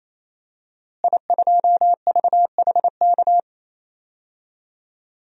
Morse code, heard as a keyed tone, about 28 wpm, 700 Hz. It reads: I2VHK
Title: DXCC-100 Call Signs - 5 Characters - Repeated 3x - 28wpm